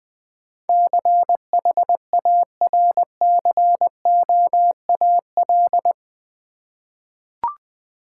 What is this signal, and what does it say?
Morse code, heard as a keyed tone, about 20 wpm, 700 Hz. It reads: CHARCOAL
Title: Easy Words with Hints - 20wpm